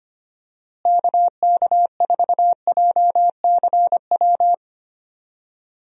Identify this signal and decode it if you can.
Morse code, heard as a keyed tone, about 25 wpm, 700 Hz. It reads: KK4JCW
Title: USA Call Signs - 6 Characters - Repeated 2x - 25wpm